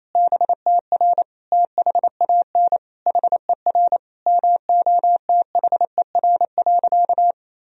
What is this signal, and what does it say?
Morse code, heard as a keyed tone, about 28 wpm, 700 Hz. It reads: BTR THAN HER MOTHER.
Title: Sentences from Top 400 Words and Abbreviations 28wpm